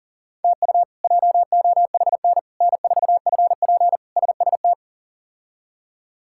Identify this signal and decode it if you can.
Morse code, heard as a keyed tone, about 40 wpm, 700 Hz. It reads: TU JOHN N4FP SST
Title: K1USN SST QSO Practice - 40wpm